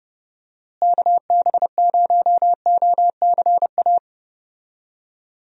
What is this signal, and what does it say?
Morse code, heard as a keyed tone, about 32 wpm, 700 Hz. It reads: KB0OCA
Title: USA Call Signs - 6 Characters - Repeated 2x - 30wpm